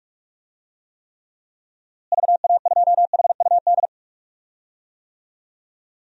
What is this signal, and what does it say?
Morse code, heard as a keyed tone, about 46 wpm, 700 Hz. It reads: VA2HUD
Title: DXCC-100 Call Signs - 6 Characters - Repeated 2x - 45wpm